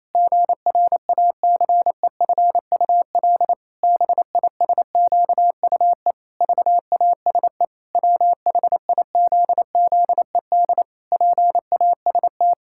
Morse code, heard as a keyed tone, about 28 wpm, 700 Hz. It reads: GRACEFUL BISQUE VASE WHIZZED PAST
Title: Pangram Challenge v2 - 28wpm